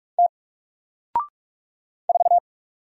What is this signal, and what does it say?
Morse code, heard as a keyed tone, about 44 wpm, 700 Hz. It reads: T 4
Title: Single Letter-Number Alt 45wpm